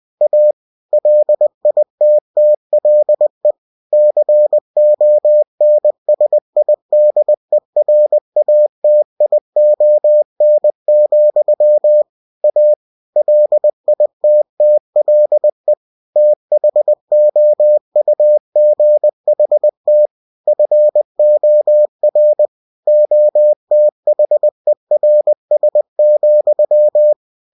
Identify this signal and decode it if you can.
Morse code, heard as a keyed tone, about 20 wpm, 600 Hz. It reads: A LITTLE CONSIDERATION, A LITTLE THOUGHT FOR OTHERS,
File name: cw-books / winnie-the-pooh / winnie-20wpm-page-0084